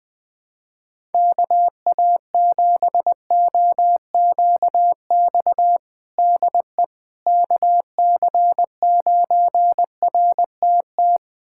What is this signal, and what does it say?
Morse code, heard as a keyed tone, about 21 wpm, 700 Hz. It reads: KA7OQX DE KC9RTT
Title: USA Callsign DE Callsign - 6 Characters - 20wpm